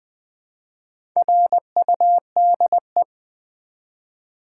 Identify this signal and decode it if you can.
Morse code, heard as a keyed tone, about 20 wpm, 700 Hz. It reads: RUDE
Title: Build A Word - 1 Letter At A Time - 4-Letter Words - 20wpm